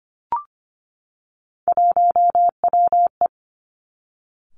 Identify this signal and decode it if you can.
Morse code, heard as a keyed tone, about 25 wpm, 700 Hz. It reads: 1WE
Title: Three Challenging Letter-Number Combinations 25wpm